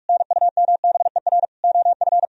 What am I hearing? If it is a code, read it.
Morse code, heard as a keyed tone, about 49 wpm, 700 Hz. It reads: NUMBER OF